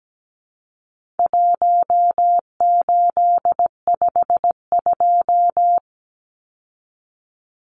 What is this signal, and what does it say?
Morse code, heard as a keyed tone, about 17 wpm, 700 Hz. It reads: 1852
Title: Numbers - 4-Digit 17wpm